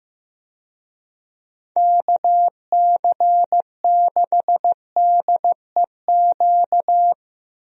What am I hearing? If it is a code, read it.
Morse code, heard as a keyed tone, about 15 wpm, 700 Hz. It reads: KC6DEQ
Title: USA Call Signs - 6 Characters - Repeated 2x - 15wpm